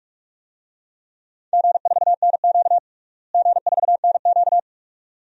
Morse code, heard as a keyed tone, about 46 wpm, 700 Hz. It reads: G4NQ G4NQ
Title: DXCC-100 Call Signs - 4 Characters - Repeated 2x - 45wpm